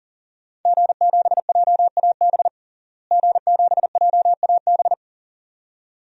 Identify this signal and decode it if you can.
Morse code, heard as a keyed tone, about 40 wpm, 700 Hz. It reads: G7JAB G7JAB
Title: DXCC-100 Call Signs - 5 Characters - Repeated 2x - 40wpm